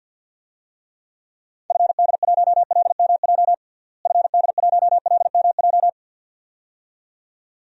Morse code, heard as a keyed tone, about 53 wpm, 700 Hz. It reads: FD1LMJ FD1LMJ
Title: DXCC-100 Call Signs - 6 Characters - Repeated 2x - 50wpm